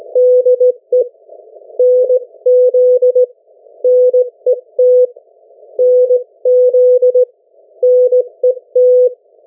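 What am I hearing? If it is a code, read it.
Morse code, heard as a keyed tone, about 17 wpm, 500 Hz. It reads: DE NZ NET NZ NET
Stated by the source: Morse code from the national amateur radio CW net of New Zealand, which meets on 3535.0 kHz or 7030 kHz.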